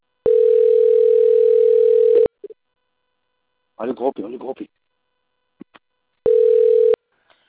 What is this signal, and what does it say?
Ringback tone (North America), 440 + 480 Hz, with speech.
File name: out- - - - - .9509